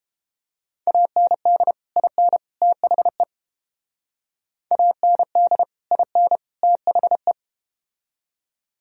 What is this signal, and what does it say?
Morse code, heard as a keyed tone, about 31 wpm, 700 Hz. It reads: AND IN THE AND IN THE